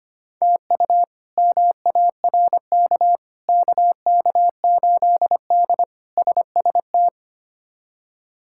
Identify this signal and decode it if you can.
Morse code, heard as a keyed tone, about 25 wpm, 700 Hz. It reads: TU MARK KK8D SST
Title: K1USN SST QSO Practice - 25wpm